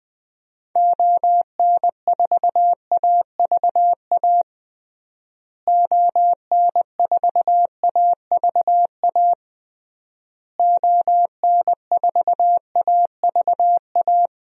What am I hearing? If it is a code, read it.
Morse code, heard as a keyed tone, about 20 wpm, 700 Hz. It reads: ON4AVA ON4AVA ON4AVA
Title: DXCC-100 Call Signs - 6 Characters - Repeated 3x - 20wpm